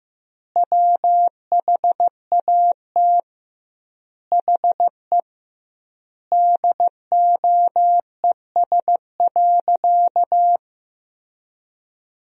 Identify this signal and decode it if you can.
Morse code, heard as a keyed tone, about 15 wpm, 700 Hz. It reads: WHAT HE DOES.
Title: 2x Word-Spacing - Sentences from Top 500 Words - 15wpm